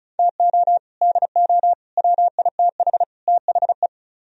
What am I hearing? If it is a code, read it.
Morse code, heard as a keyed tone, about 45 wpm, 700 Hz. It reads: TO DO WITH THE